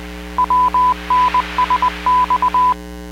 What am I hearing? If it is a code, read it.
Morse code, heard as a keyed tone, about 20 wpm, 1000 Hz, with noise. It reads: WNSX